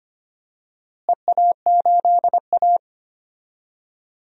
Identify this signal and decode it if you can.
Morse code, heard as a keyed tone, about 25 wpm, 700 Hz. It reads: EA8A
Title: DXCC-100 Call Signs - 4 Characters - Repeated 3x - 25wpm